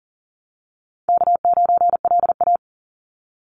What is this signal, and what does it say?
Morse code, heard as a keyed tone, about 40 wpm, 700 Hz. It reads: K9LA
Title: Easy Call Signs - 40wpm